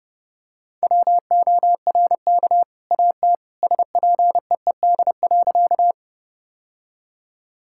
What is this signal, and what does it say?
Morse code, heard as a keyed tone, about 30 wpm, 700 Hz. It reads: WORK AT SPEED.
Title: Sentences from Top 1000 Words - Review Part 5 - 30wpm